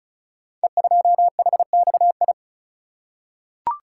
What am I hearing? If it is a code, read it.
Morse code, heard as a keyed tone, about 36 wpm, 700 Hz. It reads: E2HXI
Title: Code Groups - 5 Letter:Numbers - 35wpm